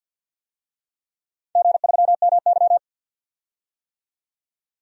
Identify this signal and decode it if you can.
Morse code, heard as a keyed tone, about 47 wpm, 700 Hz. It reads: G3MY
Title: DXCC-100 Call Signs - 4 Characters - 50wpm